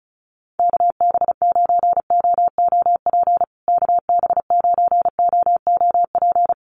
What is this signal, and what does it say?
Morse code, heard as a keyed tone, about 34 wpm, 700 Hz. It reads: KB9OOP KB9OOP
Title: USA Call Signs Repeated 3x 35wpm